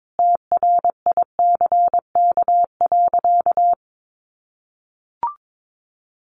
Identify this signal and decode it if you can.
Morse code, heard as a keyed tone, about 22 wpm, 700 Hz. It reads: TRICK.
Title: Sentences from Top 1500 Words 22wpm